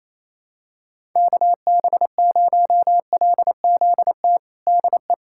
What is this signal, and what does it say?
Morse code, heard as a keyed tone, about 28 wpm, 700 Hz. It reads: KB0LZT DE
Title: USA Callsign DE Callsign - 6 Characters - 28wpm